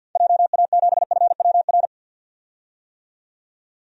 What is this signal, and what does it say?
Morse code, heard as a keyed tone, about 51 wpm, 700 Hz. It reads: JA7FWR
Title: DXCC-100 Call Signs - 6 Characters - 50wpm